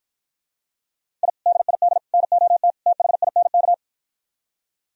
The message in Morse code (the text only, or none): I DID NOT THINK